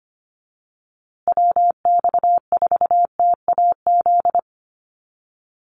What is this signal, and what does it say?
Morse code, heard as a keyed tone, about 25 wpm, 700 Hz. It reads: WX4TAZ
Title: USA Call Signs Repeated 2x 25wpm